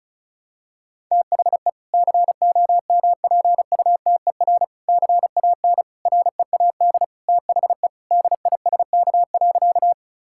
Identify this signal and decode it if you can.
Morse code, heard as a keyed tone, about 36 wpm, 700 Hz. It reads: THE COMPUTER CAN READ THE DISK.
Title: Sentences from Top 1000 Words - Review Part 6 - 35wpm